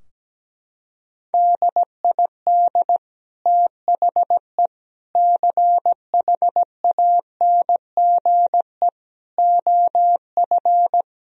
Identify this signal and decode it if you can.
Morse code, heard as a keyed tone, about 17 wpm, 700 Hz. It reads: DID THE CHANGE OF
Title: Sentences from Top 1000 Words - Review Part 4 - 17wpm